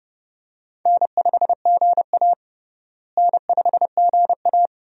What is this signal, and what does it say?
Morse code, heard as a keyed tone, about 30 wpm, 700 Hz. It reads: N5GA N5GA
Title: USA Call Signs - 4 Characters - Repeated 3x - 30wpm